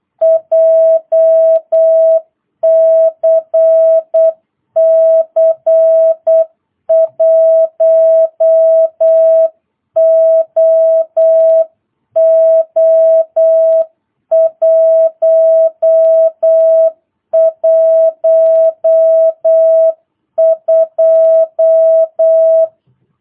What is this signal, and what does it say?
Morse code, heard as a keyed tone, about 8 wpm, 660 Hz. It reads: JCC 1OO112